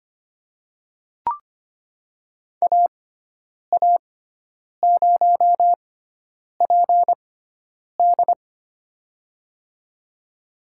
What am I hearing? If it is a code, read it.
Morse code, heard as a keyed tone, about 25 wpm, spaced as for 9 wpm, 700 Hz. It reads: AA0PD
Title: USA Call Signs - 6x Character Spacing - 25wpm